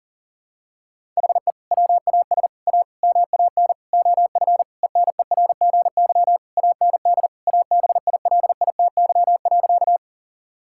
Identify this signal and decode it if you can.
Morse code, heard as a keyed tone, about 40 wpm, 700 Hz. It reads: HE WAS A MAN OF ENERGY AND ABILITY.
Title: Sentences from Top 1000 Words - Review Part 6 - 40wpm